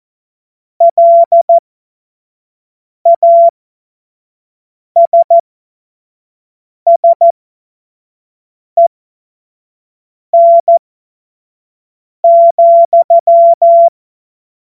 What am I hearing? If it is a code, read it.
Morse code, heard as a keyed tone, about 14 wpm, spaced as for 5 wpm, 680 Hz. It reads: LASSEN,
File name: Morsekurs / Witze / Witz 22. August 2023 25 F70